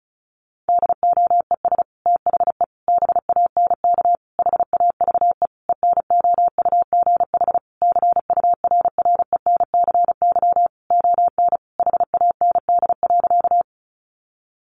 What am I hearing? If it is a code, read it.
Morse code, heard as a keyed tone, about 34 wpm, 700 Hz. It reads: DOES THE BANK HAVE ENOUGH CURRENCY ON HAND.
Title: Sentences from Top 1500 Words 35wpm